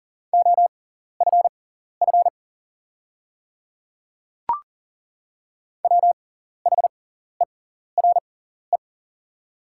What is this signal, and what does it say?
Morse code, heard as a keyed tone, about 40 wpm, spaced as for 14 wpm, 700 Hz. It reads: OFF WHERE
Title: 6x Character Spacing - Top 200 Words - 40wpm